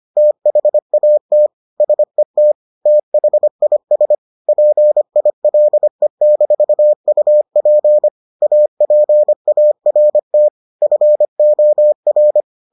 Morse code, heard as a keyed tone, about 25 wpm, 600 Hz. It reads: THAT SET THIS PILE-UP APART FOR